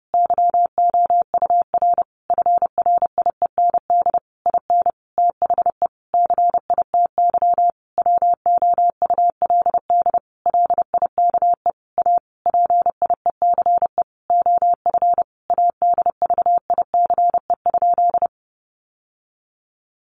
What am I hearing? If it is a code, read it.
Morse code, heard as a keyed tone, about 30 wpm, 700 Hz. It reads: YOUR FRIEND IN THE CITY WOULD LIKE A PIECE OF ADVICE?
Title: Sentences from Top 1100 Words 30wpm